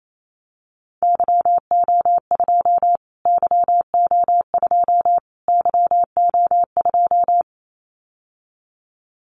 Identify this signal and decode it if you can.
Morse code, heard as a keyed tone, about 28 wpm, 700 Hz. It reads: YO2 YO2 YO2
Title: Callsign Prefixes for Top 25 DXCC - Repeated 3x - 28wpm